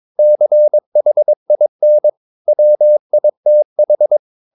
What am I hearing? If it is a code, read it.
Morse code, heard as a keyed tone, about 22 wpm, 600 Hz. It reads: CHIN WITH
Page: Romanen Never af Ken Follet på engelsk.